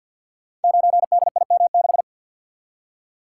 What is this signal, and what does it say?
Morse code, heard as a keyed tone, about 46 wpm, 700 Hz. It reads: 9DIM6
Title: Code Groups - 5 Letter:Numbers - 50wpm